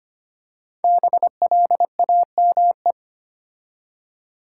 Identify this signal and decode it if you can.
Morse code, heard as a keyed tone, about 25 wpm, 700 Hz. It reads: BLAME
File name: 5 Letter Words - First 3 Letters Given - 25wpm